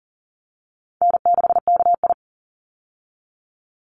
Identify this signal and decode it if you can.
Morse code, heard as a keyed tone, about 37 wpm, 700 Hz. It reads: N6KI
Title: Call Signs - 40wpm